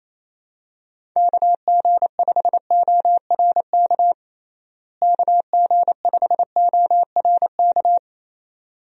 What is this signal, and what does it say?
Morse code, heard as a keyed tone, about 28 wpm, 700 Hz. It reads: KG5ORK KG5ORK
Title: USA Call Signs - 6 Characters - Repeated 2x - 28wpm